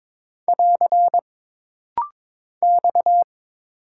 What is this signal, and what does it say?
Morse code, heard as a keyed tone, about 22 wpm, 700 Hz. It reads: <AR> X
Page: This rapid-fire timing is geared to push your ICR (Instant Character Recognition) skills.